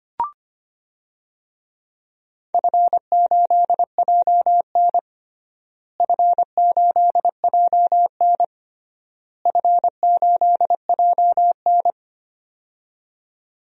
Morse code, heard as a keyed tone, about 25 wpm, 700 Hz. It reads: F8JN F8JN F8JN
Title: DXCC-100 Call Signs - 4 Characters - Repeated 3x - 25wpm